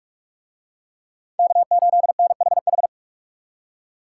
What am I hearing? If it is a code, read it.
Morse code, heard as a keyed tone, about 39 wpm, 700 Hz. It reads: K8NHH